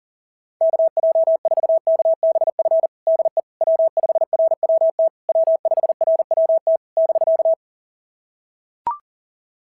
Morse code, heard as a keyed tone, about 40 wpm, 690 Hz. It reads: KJ4KBF DE W5RWT W5RWT <BK>
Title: SOTA QSO Practice 40wpm